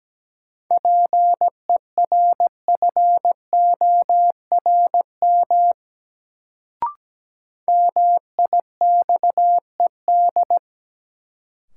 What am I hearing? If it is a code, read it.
Morse code, heard as a keyed tone, about 17 wpm, 700 Hz. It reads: PERFORM MIXED
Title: Sentences from Top 1500 Words - New Word Drill Alt - 17wpm